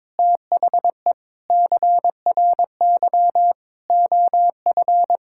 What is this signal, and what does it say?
Morse code, heard as a keyed tone, about 22 wpm, 700 Hz. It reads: THE CRY OF
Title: Sentences from Top 1000 Words - Review Part 5 - 22wpm